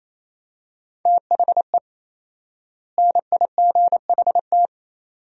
Morse code, heard as a keyed tone, about 28 wpm, 700 Hz. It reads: THE NIGHT
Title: 4x Word-Spacing - Sentences from Top 500 Words - 28wpm